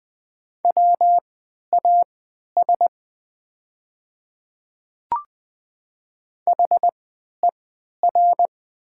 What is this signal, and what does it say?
Morse code, heard as a keyed tone, about 20 wpm, spaced as for 11 wpm, 700 Hz. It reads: WAS HER
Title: 3x Character Spacing - Top 100 Words - 20wpm